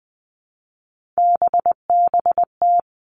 Morse code, heard as a keyed tone, about 20 wpm, 700 Hz. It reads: BBT
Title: Three Challenging Letter-Number Combinations 20wpm